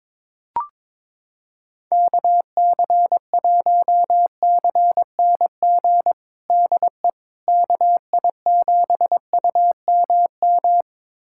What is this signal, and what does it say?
Morse code, heard as a keyed tone, about 22 wpm, 700 Hz. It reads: KC1CNG DE KI7UMM
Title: USA Callsign DE Callsign 22wpm